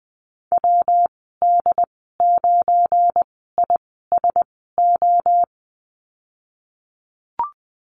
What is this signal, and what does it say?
Morse code, heard as a keyed tone, about 20 wpm, spaced as for 14 wpm, 700 Hz. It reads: WD9ISO
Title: 186 Lesson 30g - Letter b - Call Signs 20wpm